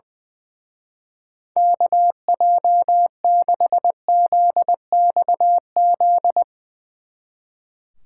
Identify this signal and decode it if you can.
Morse code, heard as a keyed tone, about 22 wpm, 700 Hz. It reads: KJ6ZXZ
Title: Speed Racing - USA 2x3 Call Signs 20wpm